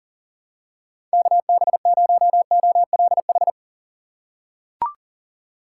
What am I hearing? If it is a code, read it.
Morse code, heard as a keyed tone, about 40 wpm, 700 Hz. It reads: KB0OLH
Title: USA Call Signs - 6x Character Spacing - 40wpm